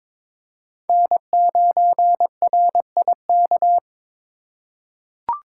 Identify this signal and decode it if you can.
Morse code, heard as a keyed tone, about 22 wpm, 700 Hz. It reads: N9RIK
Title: USA Call Signs - 5 Characters - 2x Character Spacing - 22wpm